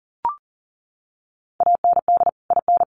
Morse code, heard as a keyed tone, about 40 wpm, 700 Hz. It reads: AND IN
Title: Sets of 2 Words 40wpm